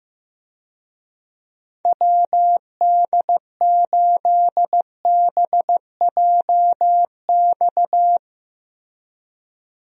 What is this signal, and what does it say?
Morse code, heard as a keyed tone, about 15 wpm, 700 Hz. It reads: WD8BJX
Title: USA Call Signs - 6 Characters - Repeated 3x - 15wpm